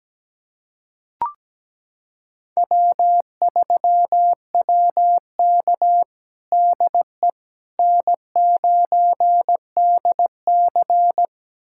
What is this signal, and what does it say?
Morse code, heard as a keyed tone, about 17 wpm, 700 Hz. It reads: W3WK DE N9DC
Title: USA Callsign DE Callsign - 4 Characters - 17wpm